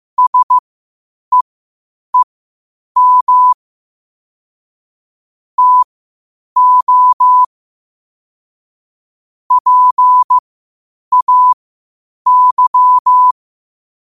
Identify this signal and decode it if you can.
Morse code, heard as a keyed tone, about 15 wpm, spaced as for 8 wpm, 1000 Hz. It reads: SEEM TO PAY
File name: qotdfiles / 8wpm